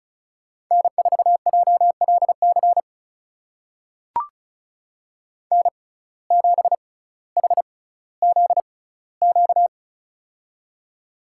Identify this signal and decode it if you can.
Morse code, heard as a keyed tone, about 36 wpm, spaced as for 18 wpm, 700 Hz. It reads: N4JLC N7HZQ
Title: USA Call Signs - 5 Characters - 6x Character Spacing - 35wpm